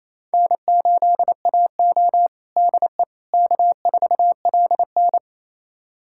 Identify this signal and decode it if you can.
Morse code, heard as a keyed tone, about 28 wpm, 700 Hz. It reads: N8AO DE K4LN
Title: USA Callsign DE Callsign - 4 Characters - 28wpm